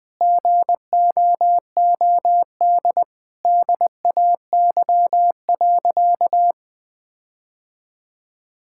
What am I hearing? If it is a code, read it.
Morse code, heard as a keyed tone, about 20 wpm, 700 Hz. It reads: GOOD DAY.
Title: Sentences with 4-Word Phrases - Spoken Beginning - 20wpm